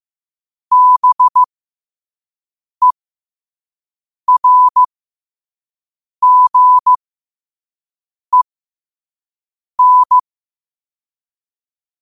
Morse code, heard as a keyed tone, about 15 wpm, spaced as for 5 wpm, 1000 Hz. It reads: BERGEN
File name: qotdfiles / 5wpm